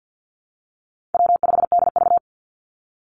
Morse code, heard as a keyed tone, about 50 wpm, 700 Hz. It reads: W5DV
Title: Call Signs Repeated 2x - 50wpm